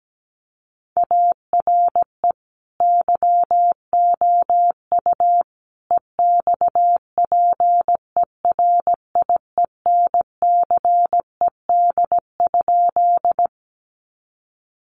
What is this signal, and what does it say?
Morse code, heard as a keyed tone, about 17 wpm, 700 Hz. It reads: ARE YOU EXPERIENCED?
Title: Sentences from Top 1100 Words 17wpm